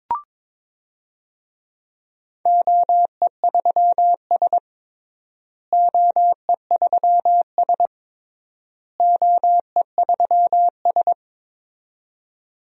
Morse code, heard as a keyed tone, about 22 wpm, 700 Hz. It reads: OE3S OE3S OE3S
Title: DXCC-100 Call Signs - 4 Characters - Repeated 3x - 22wpm